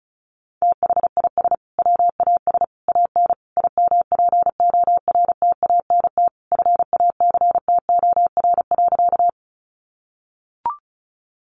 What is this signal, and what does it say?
Morse code, heard as a keyed tone, about 34 wpm, 700 Hz. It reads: THIS WAS AN IMPORTANT FACTOR.
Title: Sentences from Top 1300 Words 35wpm